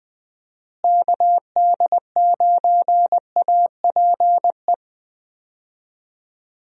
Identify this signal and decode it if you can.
Morse code, heard as a keyed tone, about 20 wpm, 700 Hz. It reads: KD9APE
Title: USA Call Signs - 6 Characters - Repeated 2x - 20wpm